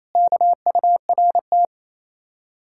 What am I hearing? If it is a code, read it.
Morse code, heard as a keyed tone, about 36 wpm, 700 Hz. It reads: KURT